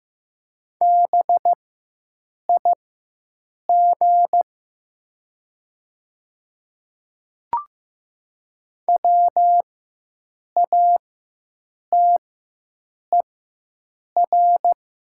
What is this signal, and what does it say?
Morse code, heard as a keyed tone, about 15 wpm, spaced as for 7 wpm, 700 Hz. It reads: BIG WATER
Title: 4x Character Spacing - Top 300 Words - 15wpm